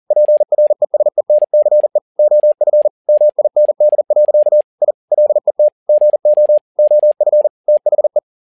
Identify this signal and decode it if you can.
Morse code, heard as a keyed tone, about 40 wpm, 600 Hz. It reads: PRESENCE OF MIND. I LET GO OF THE
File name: learning center / tower trouble / tower touble 40wpm